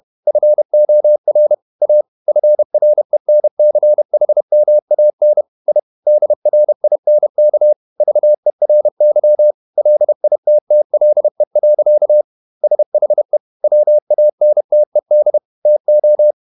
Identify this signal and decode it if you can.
Morse code, heard as a keyed tone, about 31 wpm, 600 Hz. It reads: FOR A FRENCHMAN I DRINK VERY LITTLE. SHE WANTED TO
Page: Never 31wpm | CW med Gnister